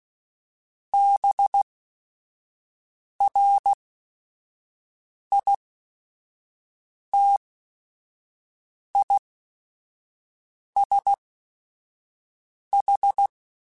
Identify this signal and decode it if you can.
Morse code, heard as a keyed tone, about 16 wpm, spaced as for 5 wpm, 800 Hz. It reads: BRITISH
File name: General / General-mp3s / ISH @5